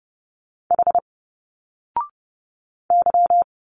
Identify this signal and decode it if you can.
Morse code, heard as a keyed tone, about 30 wpm, 700 Hz. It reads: H Y
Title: Single Letters 30wpm